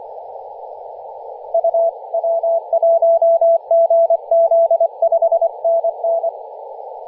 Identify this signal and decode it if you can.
Morse code, heard as a keyed tone, about 25 wpm, 670 Hz, with radio noise. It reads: UW1GZ 5NN
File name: posts / 2016-nov-10-0528z-t30ma-21-007mhz-cw